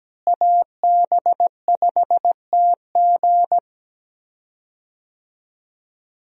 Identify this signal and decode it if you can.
Morse code, heard as a keyed tone, about 17 wpm, 700 Hz. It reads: AB5TG
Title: USA Call Signs - 5 Characters - Repeated 3x - 17wpm